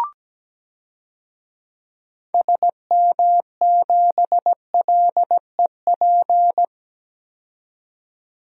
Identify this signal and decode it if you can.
Morse code, heard as a keyed tone, about 17 wpm, 700 Hz. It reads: SM7LEP
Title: DXCC-100 Call Signs - 6 Characters - 17wpm